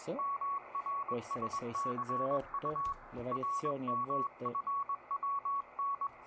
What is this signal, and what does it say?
Morse code, heard as a keyed tone, about 21 wpm, 1120 Hz, with speech and radio noise. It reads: MRSCU RMFWN